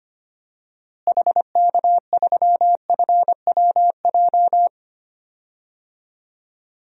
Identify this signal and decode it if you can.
Morse code, heard as a keyed tone, about 25 wpm, 700 Hz. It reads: HK3FWJ
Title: DXCC-100 Call Signs - 6 Characters - Repeated 2x - 25wpm